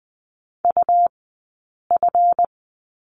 Morse code, heard as a keyed tone, about 20 wpm, spaced as for 8 wpm, 700 Hz. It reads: UF
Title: 137 Lesson 23d - Letter f - Two Characters 20wpm